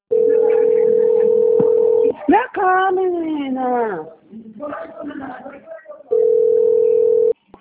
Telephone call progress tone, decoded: ringback tone

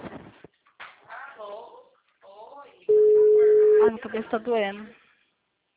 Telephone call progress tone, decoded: ringback tone

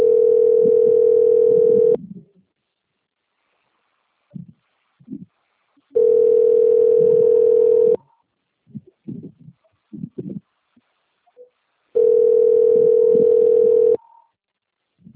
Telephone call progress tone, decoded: ringback tone

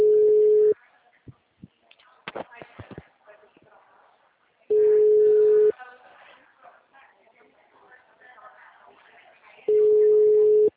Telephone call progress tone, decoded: ringback tone

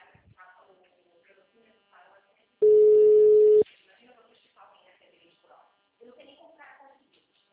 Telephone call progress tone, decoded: ringback tone

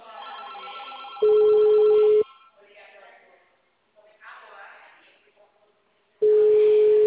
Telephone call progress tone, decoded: ringback tone

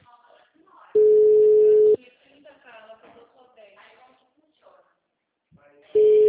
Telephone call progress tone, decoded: ringback tone

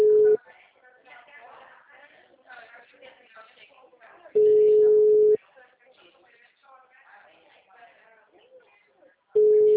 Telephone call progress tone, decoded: ringback tone

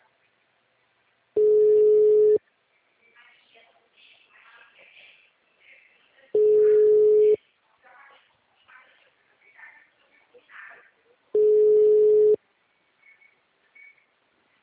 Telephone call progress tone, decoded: ringback tone

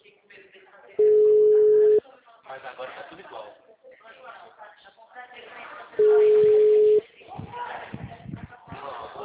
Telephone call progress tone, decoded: ringback tone

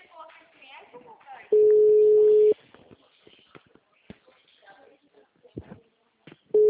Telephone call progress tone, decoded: ringback tone

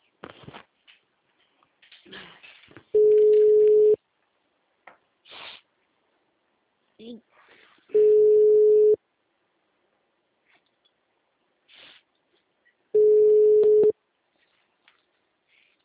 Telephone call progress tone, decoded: ringback tone

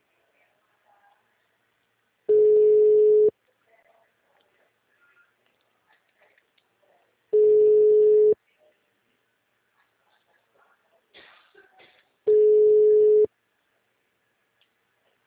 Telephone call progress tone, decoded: ringback tone

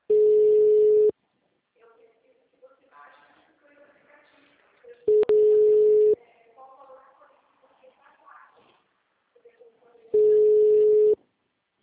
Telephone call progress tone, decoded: ringback tone

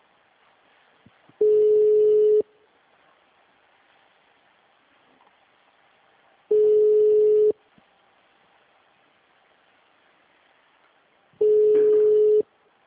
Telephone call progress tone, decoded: ringback tone

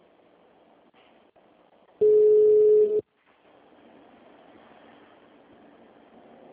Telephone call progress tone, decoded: ringback tone

